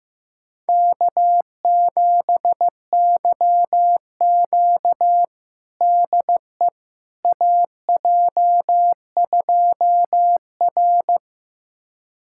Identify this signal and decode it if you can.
Morse code, heard as a keyed tone, about 15 wpm, 700 Hz. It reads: K7YQ DE AJ2R
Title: USA Callsign DE Callsign - 4 Characters - 15wpm